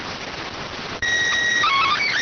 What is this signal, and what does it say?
SSTV (slow-scan television), PD 290.